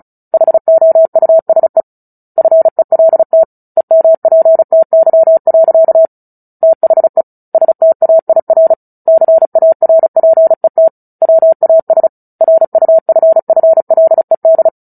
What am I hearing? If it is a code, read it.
Morse code, heard as a keyed tone, about 35 wpm, 650 Hz. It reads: HOUSE FELT EMPTY. THE STAIR CARPET WAS RUFFLED
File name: Book Two / 09-Chapter 9 - 35 WPM